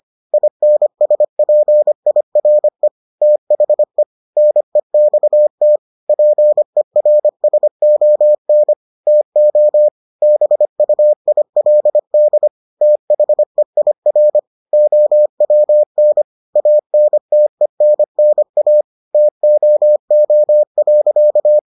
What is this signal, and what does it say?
Morse code, heard as a keyed tone, about 25 wpm, 600 Hz. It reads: INSPIRE THE NEXT PERSON TO BUILD THEIR OWN ANTENNA TOO.